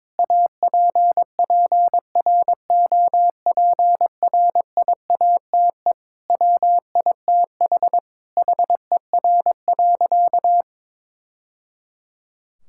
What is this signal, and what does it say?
Morse code, heard as a keyed tone, about 22 wpm, 700 Hz. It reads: APPROPRIATE WITH HER.
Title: Sentences from Top 1000 Words - Review Part 5 - 22wpm